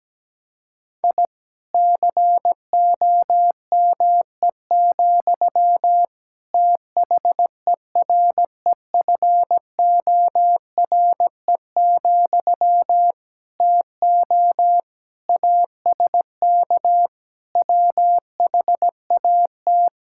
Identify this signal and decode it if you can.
Morse code, heard as a keyed tone, about 17 wpm, 700 Hz. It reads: I COME, THEREFORE, TO ASK WHAT
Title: Sentences from Top 1000 Words - Review Part 5 - 17wpm